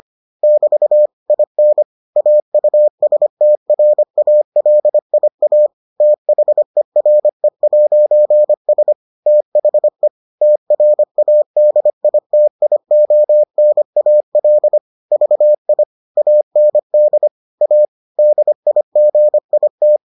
Morse code, heard as a keyed tone, about 25 wpm, 600 Hz. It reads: <BT> IN AUSTRALIA THERE'S THE TRADITIONAL VI AND A DIGIT